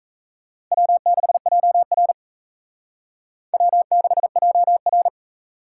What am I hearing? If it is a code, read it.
Morse code, heard as a keyed tone, about 34 wpm, 700 Hz. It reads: W6JR W6JR
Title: Speed Racing - USA 1x2 Call Signs 28wpm